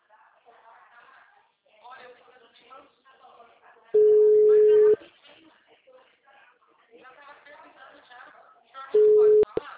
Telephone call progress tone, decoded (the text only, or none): ringback tone